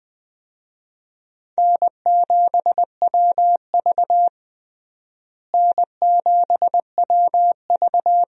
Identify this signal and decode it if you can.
Morse code, heard as a keyed tone, about 21 wpm, 700 Hz. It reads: N7WV N7WV
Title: USA Call Signs - 4 Characters - Repeated 2x - 20wpm